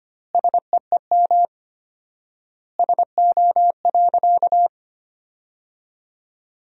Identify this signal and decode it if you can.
Morse code, heard as a keyed tone, about 25 wpm, 700 Hz. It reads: SEEM SO.
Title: 4x Word-Spacing - Sentences from Top 300 Words - 25wpm